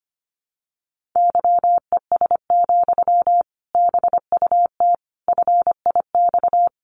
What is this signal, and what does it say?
Morse code, heard as a keyed tone, about 25 wpm, 700 Hz. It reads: YES, BUT FIX
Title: Sentences from Top 1400 Words 25wpm